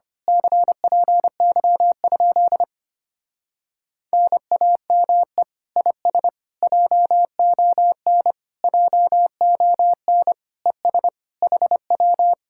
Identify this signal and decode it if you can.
Morse code, heard as a keyed tone, about 26 wpm, 700 Hz. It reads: CPY? NAME IS JON JON ES HW
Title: Speed-Racing Common QSO Phrases - 25wpm